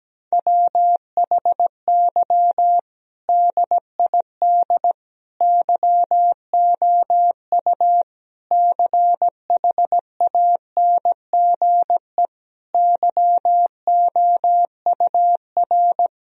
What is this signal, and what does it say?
Morse code, heard as a keyed tone, about 17 wpm, 700 Hz. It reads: WHY DID YOU CHANGE YOUR
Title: Sentences from Top 1000 Words - Review Part 1 - 17wpm